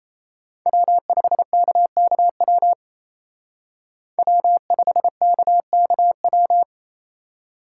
Morse code, heard as a keyed tone, about 36 wpm, 700 Hz. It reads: W5KKW W5KKW